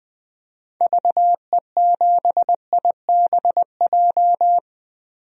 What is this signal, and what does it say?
Morse code, heard as a keyed tone, about 20 wpm, 700 Hz. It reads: VE7IBJ
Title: DXCC-100 Call Signs - 6 Characters - Repeated 3x - 20wpm